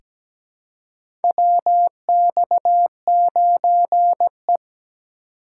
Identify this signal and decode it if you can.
Morse code, heard as a keyed tone, about 17 wpm, 700 Hz. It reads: WX9E
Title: USA Call Signs - 4x Character Spacing - 17wpm